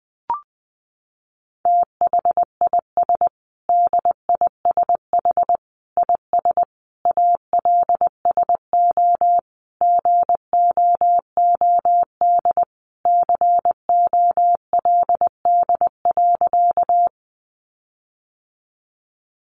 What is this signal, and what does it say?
Morse code, heard as a keyed tone, about 20 wpm, 700 Hz. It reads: THIS DISH IS ALSO GOOD COLD.
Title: Sentences from Top 1300 Words 20wpm